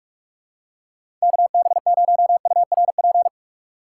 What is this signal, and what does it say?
Morse code, heard as a keyed tone, about 48 wpm, 700 Hz. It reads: KB0URP
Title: USA Call Signs - 6 Characters - 6x Character Spacing - 45wpm